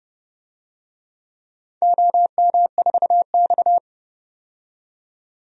Morse code, heard as a keyed tone, about 32 wpm, 700 Hz. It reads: OM4X
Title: DXCC-100 Call Signs - 4 Characters - 30wpm